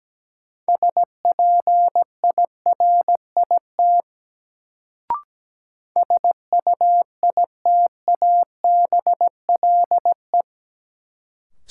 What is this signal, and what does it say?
Morse code, heard as a keyed tone, about 17 wpm, 700 Hz. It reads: SPIRIT SUITABLE
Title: Sentences from Top 1400 Words - New Word Drill Alt - 17wpm